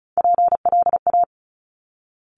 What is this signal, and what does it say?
Morse code, heard as a keyed tone, about 34 wpm, 700 Hz. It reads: PLA
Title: Top Three Letter Combinations in Words 35wpm